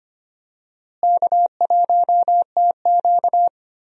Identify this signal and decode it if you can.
Morse code, heard as a keyed tone, about 25 wpm, 700 Hz. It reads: K1TQ
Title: USA Call Signs - 4 Characters - Repeated 2x - 25wpm